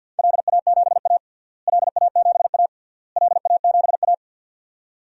Morse code, heard as a keyed tone, about 48 wpm, 700 Hz. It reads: LA7A LA7A LA7A
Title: DXCC-100 Call Signs - 4 Characters - Repeated 3x - 50wpm